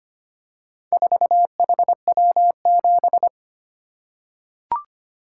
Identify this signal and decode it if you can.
Morse code, heard as a keyed tone, about 25 wpm, 700 Hz. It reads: 4HW7
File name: practice / 25wpm / Four Letter-Number Combinations - 25wpm